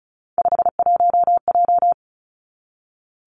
Morse code, heard as a keyed tone, about 36 wpm, 700 Hz. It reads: 51J
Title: Three Challenging Letter-Number Combinations 35wpm